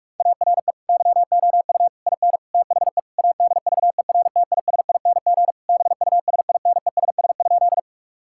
Morse code, heard as a keyed tone, about 46 wpm, 700 Hz. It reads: ARE YOU IN THE ADVERTISING BUSINESS?
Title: Sentences from Top 1000 Words - Review Part 6 - 45wpm